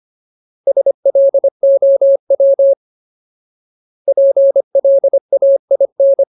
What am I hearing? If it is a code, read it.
Morse code, heard as a keyed tone, about 25 wpm, 550 Hz. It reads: SLOW PLAIN
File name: Ultimate Word List / 25WPM Ultimate